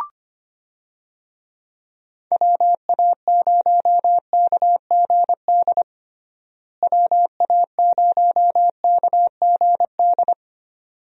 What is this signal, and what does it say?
Morse code, heard as a keyed tone, about 25 wpm, 700 Hz. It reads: WA0KGD WA0KGD
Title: USA Call Signs - 6 Characters - Repeated 2x - 25wpm